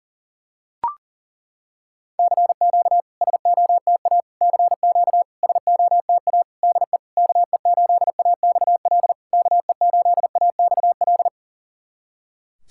Morse code, heard as a keyed tone, about 38 wpm, 690 Hz. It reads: CQ SOTA CQ SOTA DE KE8AXL KE8AXL
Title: SOTA QSO Practice 40wpm